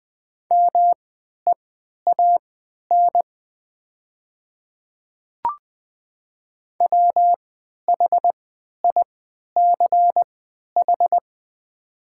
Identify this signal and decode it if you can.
Morse code, heard as a keyed tone, about 20 wpm, spaced as for 11 wpm, 700 Hz. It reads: MEAN WHICH
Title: 3x Character Spacing - Top 500 Words - 20wpm